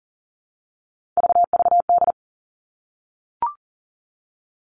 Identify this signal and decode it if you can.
Morse code, heard as a keyed tone, about 40 wpm, 700 Hz. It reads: VVD
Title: Three Challenging Letter-Number Combinations 40wpm